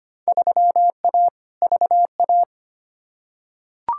Morse code, heard as a keyed tone, about 25 wpm, 700 Hz. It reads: 3A VA
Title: Field Day Exchanges Encore - 25wpm